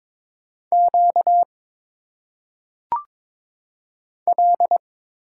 Morse code, heard as a keyed tone, about 22 wpm, 700 Hz. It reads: Q L
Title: Single Letters - Phonetic Pronunciation 22wpm